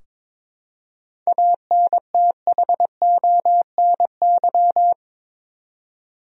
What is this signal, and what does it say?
Morse code, heard as a keyed tone, about 22 wpm, 700 Hz. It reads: ANTHONY
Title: USA Common Names - 22wpm